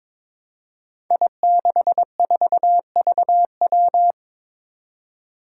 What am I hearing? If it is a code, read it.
Morse code, heard as a keyed tone, about 22 wpm, 700 Hz. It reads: I64VW
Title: Five Challenging Letter-Number Combinations 22wpm